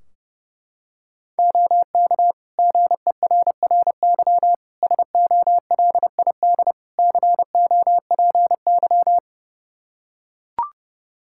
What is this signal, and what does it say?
Morse code, heard as a keyed tone, about 30 wpm, 700 Hz. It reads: OK GERRY SOLID COPY
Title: Practice QSO - Exchange 3 - More Info - 30wpm